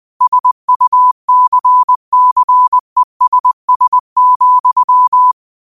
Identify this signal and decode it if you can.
Morse code, heard as a keyed tone, about 20 wpm, 1000 Hz. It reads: SUCCESS,
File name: qotdfiles / 20wpm